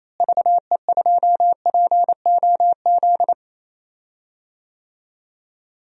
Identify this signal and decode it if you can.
Morse code, heard as a keyed tone, about 28 wpm, 700 Hz. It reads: VE2POZ
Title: DXCC-100 Call Signs - 6 Characters - Repeated 2x - 28wpm